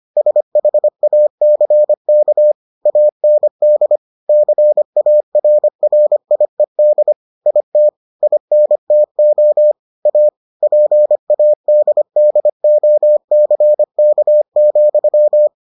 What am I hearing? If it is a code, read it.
Morse code, heard as a keyed tone, about 25 wpm, 600 Hz. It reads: SHACK AND CARRIED IT INTO A PADDOCK,